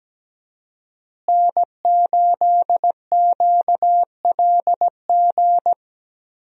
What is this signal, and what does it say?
Morse code, heard as a keyed tone, about 17 wpm, 700 Hz. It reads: N8QLG
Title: USA Call Signs - 5 Characters - Repeated 2x - 17wpm